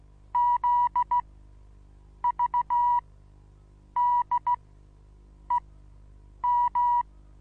Morse code, heard as a keyed tone, about 16 wpm, spaced as for 7 wpm, 970 Hz. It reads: ZVDEM
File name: Onlinelehrgang / morsen / DJ4UF Morsen25